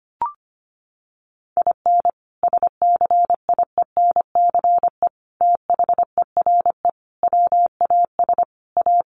Morse code, heard as a keyed tone, about 25 wpm, 700 Hz. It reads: IN SCIENCE THERE WAS A
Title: Sentences from Top 1400 Words 25wpm